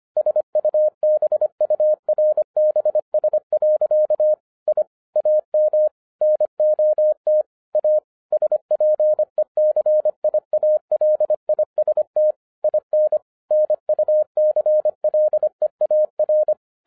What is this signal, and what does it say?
Morse code, heard as a keyed tone, about 25 wpm, 610 Hz. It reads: SUBURBS. I AM NOT A SPECIALIST IN NUCLEAR
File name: 2011-07-mp3 / n39 THE 3.11 2 25wpm